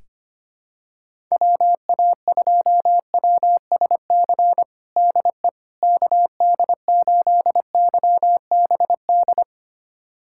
Morse code, heard as a keyed tone, about 25 wpm, 700 Hz. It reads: WA2WSC DE KD8YBD
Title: USA Callsign DE Callsign - 6 Characters - 25wpm